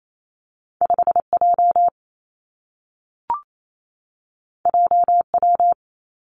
Morse code, heard as a keyed tone, about 28 wpm, 700 Hz. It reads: 5 J J W
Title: Two Challenging Letter-Number Combinations 28wpm